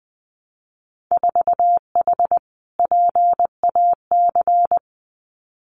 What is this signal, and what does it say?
Morse code, heard as a keyed tone, about 20 wpm, 700 Hz. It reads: 4H PAC
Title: Winter Field Day Exchanges Encore 20wpm